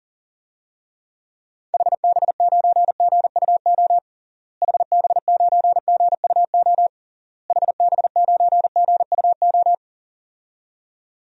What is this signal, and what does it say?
Morse code, heard as a keyed tone, about 40 wpm, 700 Hz. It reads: HB9GUO HB9GUO HB9GUO
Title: DXCC-100 Call Signs - 6 Characters - Repeated 3x - 40wpm